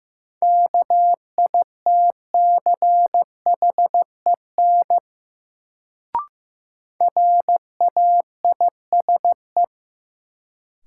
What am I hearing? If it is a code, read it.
Morse code, heard as a keyed tone, about 15 wpm, 700 Hz. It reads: KITCHEN RAISE
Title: Sentences from Top 1400 Words - New Word Drill Alt - 15wpm